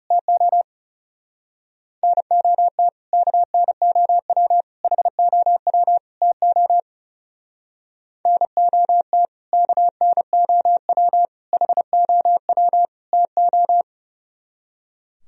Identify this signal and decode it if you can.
Morse code, heard as a keyed tone, about 34 wpm, 700 Hz. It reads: TO NOT KNOW HOW TO NOT KNOW HOW TO
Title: Speed-Racing - Sets of 4 Words 30wpm